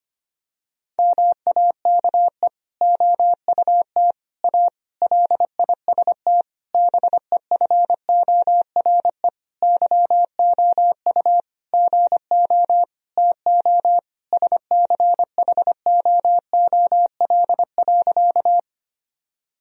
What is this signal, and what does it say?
Morse code, heard as a keyed tone, about 25 wpm, 700 Hz. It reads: MAKE OUT A LIST BEFORE YOU GO TO SCHOOL.
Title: Sentences from Top 1000 Words - Review Part 2 - 25wpm